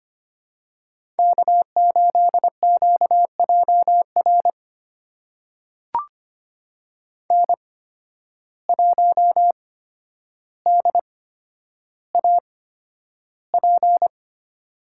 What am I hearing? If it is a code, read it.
Morse code, heard as a keyed tone, about 25 wpm, spaced as for 11 wpm, 700 Hz. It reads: K8QJR N1DAP
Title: USA Call Signs - 5 Characters - 8x Character Spacing - 25wpm